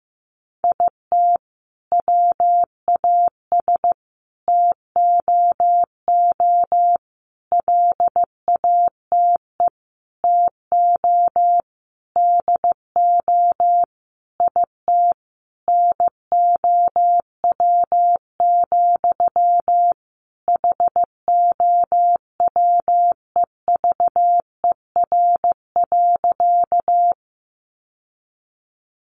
Morse code, heard as a keyed tone, about 15 wpm, 700 Hz. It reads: IT WAS TOO LATE TO DO IT NOW, HOWEVER.
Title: Sentences from Top 600 Words 15wpm